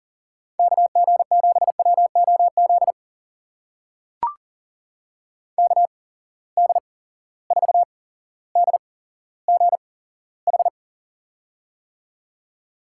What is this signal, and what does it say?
Morse code, heard as a keyed tone, about 39 wpm, spaced as for 17 wpm, 700 Hz. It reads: KG7WOZ KD4DGH
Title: USA Call Signs - 6 Characters - 8x Character Spacing - 40wpm